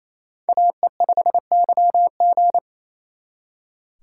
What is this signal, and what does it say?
Morse code, heard as a keyed tone, about 35 wpm, 700 Hz. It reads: AE5YG